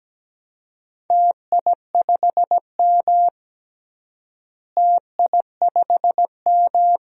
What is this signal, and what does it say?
Morse code, heard as a keyed tone, about 17 wpm, 700 Hz. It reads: TI5M TI5M
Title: DXCC-100 Call Signs - 4 Characters - Repeated 2x - 17wpm